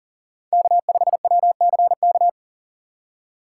Morse code, heard as a keyed tone, about 42 wpm, 700 Hz. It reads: K5WCK
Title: USA Call Signs - 5 Characters - 6x Character Spacing - 40wpm